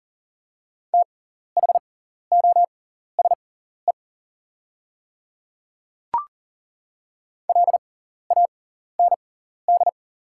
Morse code, heard as a keyed tone, about 40 wpm, spaced as for 14 wpm, 700 Hz. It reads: THOSE LAND
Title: 6x Character Spacing - Top 300 Words - 40wpm